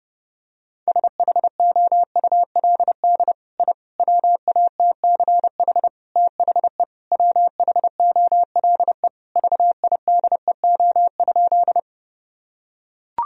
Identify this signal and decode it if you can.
Morse code, heard as a keyed tone, about 32 wpm, 700 Hz. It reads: SHOULD I WATCH THE WHOLE VIDEO?
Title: Sentences from Top 1000 Words - Review Part 4 - 30wpm